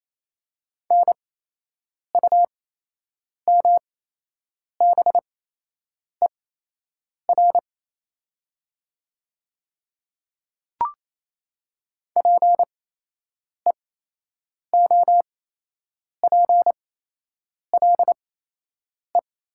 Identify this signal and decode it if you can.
Morse code, heard as a keyed tone, about 28 wpm, spaced as for 8 wpm, 700 Hz. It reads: NUMBER PEOPLE
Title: 8x Character Spacing - Top 100 Words - 28wpm